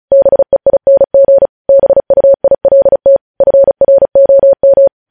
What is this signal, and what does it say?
Morse code, heard as a keyed tone, about 36 wpm, 550 Hz. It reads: BEING BUILT FROM
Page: News Headlines in Morse Code at 35 WPM